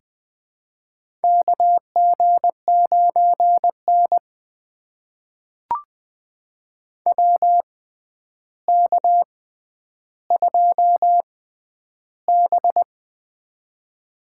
Morse code, heard as a keyed tone, about 20 wpm, spaced as for 10 wpm, 700 Hz. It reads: KG9N WK2B
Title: USA Call Signs - 4 Characters - 6x Character Spacing - 20wpm